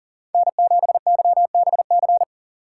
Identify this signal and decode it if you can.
Morse code, heard as a keyed tone, about 40 wpm, 700 Hz. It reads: N7YBC
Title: USA Call Signs - 5 Characters - 3x Character Spacing - 40wpm